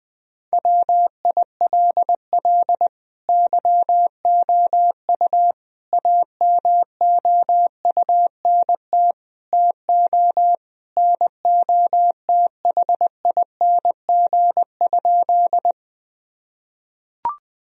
Morse code, heard as a keyed tone, about 21 wpm, 700 Hz. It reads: WILL YOU AMOUNT TO NOTHING?
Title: Sentences from Top 1000 Words - Review Part 4 - 20wpm